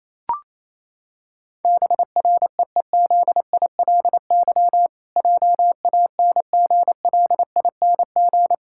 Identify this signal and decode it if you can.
Morse code, heard as a keyed tone, about 28 wpm, 700 Hz. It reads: BREEZILY JANGLING
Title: Pangram Challenge v2 - 28wpm